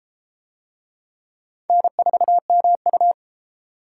Morse code, heard as a keyed tone, about 30 wpm, 700 Hz. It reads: N4MU